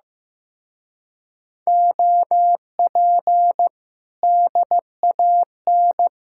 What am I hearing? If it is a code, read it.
Morse code, heard as a keyed tone, about 16 wpm, 700 Hz. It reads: OP DAN
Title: Speed-Racing Common QSO Phrases Encore - 15wpm